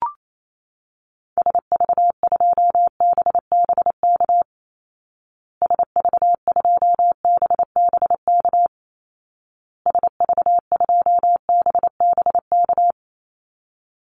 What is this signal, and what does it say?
Morse code, heard as a keyed tone, about 28 wpm, 700 Hz. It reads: SV2BBK SV2BBK SV2BBK
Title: Call Signs Repeated 2x - 28wpm